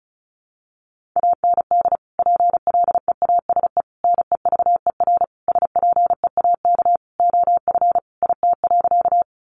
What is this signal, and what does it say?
Morse code, heard as a keyed tone, about 34 wpm, 700 Hz. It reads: AND PLEASE NEVER SPEAK OF IT.
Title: Sentences from Top 1500 Words 35wpm